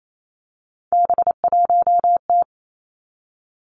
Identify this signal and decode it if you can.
Morse code, heard as a keyed tone, about 28 wpm, 690 Hz. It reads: B1T
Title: Three Challenging Letter-Number Combinations 28wpm